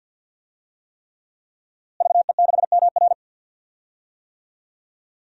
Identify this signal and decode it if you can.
Morse code, heard as a keyed tone, about 48 wpm, 700 Hz. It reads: VE6MR
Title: DXCC-100 Call Signs - 5 Characters - Repeated 3x - 50wpm